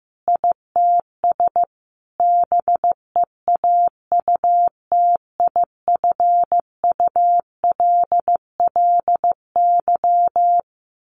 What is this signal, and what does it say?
Morse code, heard as a keyed tone, about 15 wpm, 700 Hz. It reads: ITS BEAUTIFULLY
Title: A Sound Of Thunder 15wpm